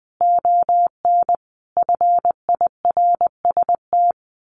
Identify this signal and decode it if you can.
Morse code, heard as a keyed tone, about 19 wpm, 700 Hz. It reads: ON FIRST
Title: A Sound Of Thunder 20wpm